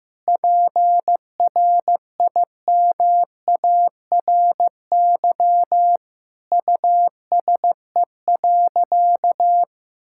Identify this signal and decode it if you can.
Morse code, heard as a keyed tone, about 15 wpm, 700 Hz. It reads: PRIMARY USE.
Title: Sentences from Top 1000 Words - Review Part 5 - 15wpm